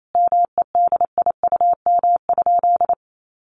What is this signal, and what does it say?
Morse code, heard as a keyed tone, about 28 wpm, 700 Hz. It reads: MEDIUM?
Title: Sentences from Top 900 Words 28wpm